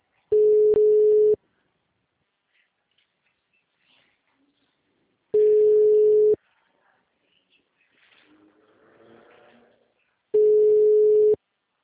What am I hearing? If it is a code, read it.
Ringback tone (Europe, 425 Hz).
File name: out- -2015- - - .17761